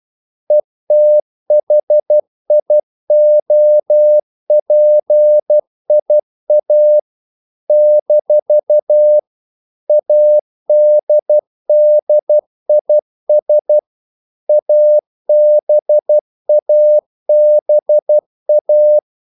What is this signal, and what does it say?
Morse code, heard as a keyed tone, about 12 wpm, 600 Hz. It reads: ETHIOPIA - ADDIS ABABA
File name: CW / Practice / NationalCapitals12